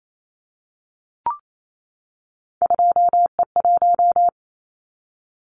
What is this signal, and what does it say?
Morse code, heard as a keyed tone, about 28 wpm, 700 Hz. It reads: 2E1
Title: Callsign Prefixes for Top 25 DXCC - 28wpm